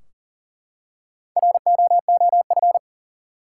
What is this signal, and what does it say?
Morse code, heard as a keyed tone, about 40 wpm, 700 Hz. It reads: ROOF
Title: Four Letter Words - 40wpm